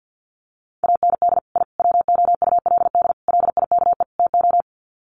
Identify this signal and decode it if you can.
Morse code, heard as a keyed tone, about 50 wpm, 700 Hz. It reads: AND I WOULD LIKE TO
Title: Sets of 5 Words Encore 50wpm